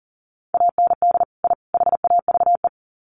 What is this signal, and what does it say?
Morse code, heard as a keyed tone, about 40 wpm, 700 Hz. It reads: AND I HAVE
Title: Sets of 3 Words 40wpm